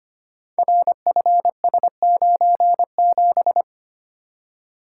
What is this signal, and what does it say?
Morse code, heard as a keyed tone, about 25 wpm, 700 Hz. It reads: RFS97
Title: Code Groups - 5 Letter:Numbers - 25wpm